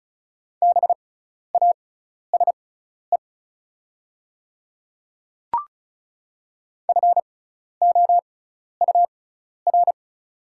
Morse code, heard as a keyed tone, about 34 wpm, spaced as for 12 wpm, 690 Hz. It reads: BASE FOUR
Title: 6x Character Spacing - Top 500 Words - 35wpm